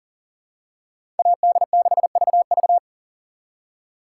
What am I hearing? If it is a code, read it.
Morse code, heard as a keyed tone, about 37 wpm, 700 Hz. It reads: AD6VV
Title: USA Call Signs - 5 Characters - 2x Character Spacing - 40wpm